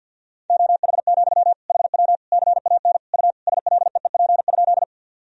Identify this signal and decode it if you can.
Morse code, heard as a keyed tone, about 52 wpm, 700 Hz. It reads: OH, HW CAN U SLEEP?
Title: Sentences from Top 500 Words and Abbreviations 50wpm